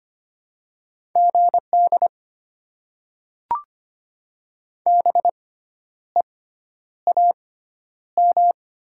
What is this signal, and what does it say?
Morse code, heard as a keyed tone, about 25 wpm, spaced as for 10 wpm, 690 Hz. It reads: GD BEAM
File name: Top 90 QSO Elements - 6x Character Spacing - 25wpm